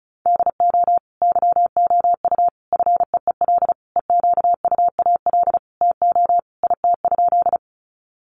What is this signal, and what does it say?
Morse code, heard as a keyed tone, about 34 wpm, 700 Hz. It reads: DO YOU FEEL EQUAL TO IT?
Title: Sentences from Top 1400 Words 35wpm